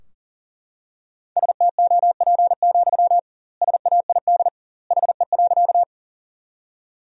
Morse code, heard as a keyed tone, about 40 wpm, 700 Hz. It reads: STOP, SAID HE.
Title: 2x Word-Spacing - Sentences from Top 300 Words - 40wpm